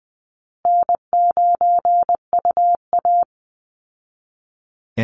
Morse code, heard as a keyed tone, about 20 wpm, 700 Hz. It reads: N9UA
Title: Easy Call Signs Repeated 2x - 20wpm